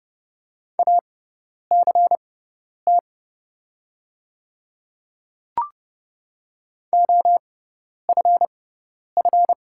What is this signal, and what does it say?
Morse code, heard as a keyed tone, about 30 wpm, spaced as for 10 wpm, 700 Hz. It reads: ACT OFF
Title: 6x Character Spacing - Top 300 Words - 30wpm